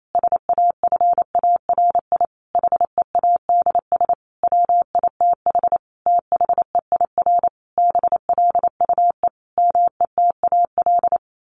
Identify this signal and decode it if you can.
Morse code, heard as a keyed tone, about 28 wpm, 700 Hz. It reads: SAFARI HEADS WITH THEIR BLUE METAL
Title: A Sound Of Thunder 28wpm